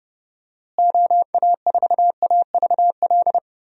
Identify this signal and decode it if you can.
Morse code, heard as a keyed tone, about 30 wpm, 700 Hz. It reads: OA4AVL
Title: USA Call Signs - 6 Characters - Repeated 3x - 30wpm